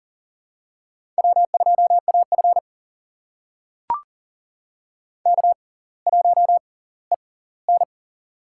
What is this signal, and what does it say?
Morse code, heard as a keyed tone, about 40 wpm, spaced as for 20 wpm, 700 Hz. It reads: W2AF K1EN
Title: USA Call Signs - 4 Characters - 6x Character Spacing - 40wpm